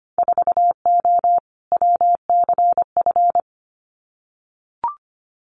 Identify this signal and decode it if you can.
Morse code, heard as a keyed tone, about 25 wpm, 700 Hz. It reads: 4O WCF
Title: Winter Field Day Exchanges Encore 25wpm